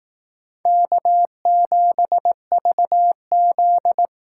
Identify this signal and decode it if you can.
Morse code, heard as a keyed tone, about 18 wpm, 700 Hz. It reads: K7VZ
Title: Speed Racing - USA 1x2 Call Signs 15wpm